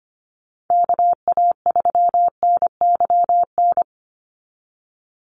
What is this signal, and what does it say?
Morse code, heard as a keyed tone, about 25 wpm, 700 Hz. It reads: KA3NYN
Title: USA Call Signs 25wpm